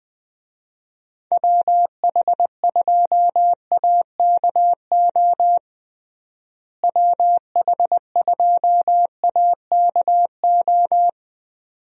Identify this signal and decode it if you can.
Morse code, heard as a keyed tone, about 21 wpm, 700 Hz. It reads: WH2AKO WH2AKO
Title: USA Call Signs - 6 Characters - Repeated 2x - 20wpm